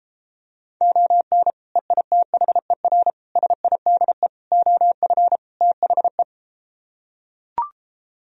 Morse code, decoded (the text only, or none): ON EITHER SIDE OF THE